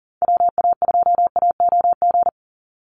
Morse code, heard as a keyed tone, about 40 wpm, 700 Hz. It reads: WA2AOG
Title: Call Signs - 40wpm